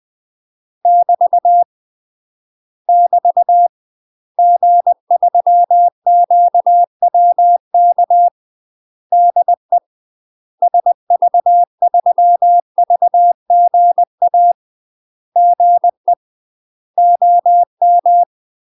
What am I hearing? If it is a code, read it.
Morse code, heard as a keyed tone, about 20 wpm, 700 Hz. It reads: <BT> <BT> G3QWK DE SV3VGA GE OM
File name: UK MASTER QSOs TRAINING / 20WPM / UK MASTER QSOs 0059